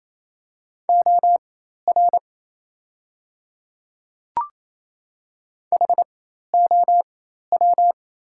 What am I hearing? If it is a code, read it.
Morse code, heard as a keyed tone, about 28 wpm, spaced as for 13 wpm, 700 Hz. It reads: OR HOW
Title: 4x Character Spacing - Top 300 Words - 28wpm